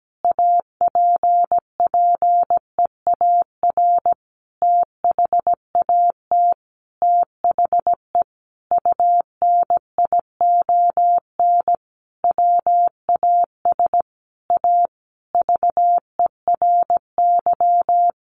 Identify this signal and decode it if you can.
Morse code, heard as a keyed tone, about 17 wpm, 700 Hz. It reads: APPEAR THAT THE UNION WAS A VERY
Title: Sentences from Top 1400 Words 17wpm